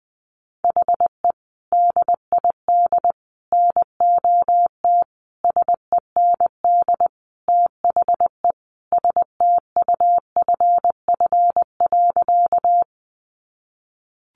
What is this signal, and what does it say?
Morse code, heard as a keyed tone, about 20 wpm, 700 Hz. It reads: HE DID NOT SEND THE STUFF.
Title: Sentences from Top 1500 Words 20wpm